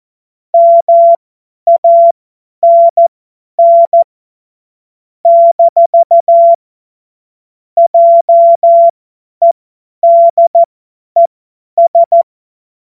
Morse code, heard as a keyed tone, about 14 wpm, spaced as for 10 wpm, 680 Hz. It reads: MANN - JEDES
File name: Morsekurs / Witze / Witz 03. Januar 2023 50 F70